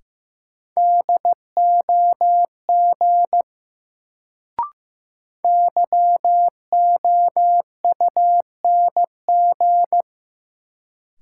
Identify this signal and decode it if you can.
Morse code, heard as a keyed tone, about 15 wpm, 700 Hz. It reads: DOG YOUNG
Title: Sentences from Top 400 Words - New Word Drill - 15wpm